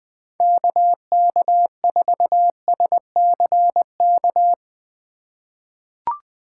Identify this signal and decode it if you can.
Morse code, heard as a keyed tone, about 20 wpm, 700 Hz. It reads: KK4SCK
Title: USA Call Signs - 6 Characters - 3x Character Spacing - 20wpm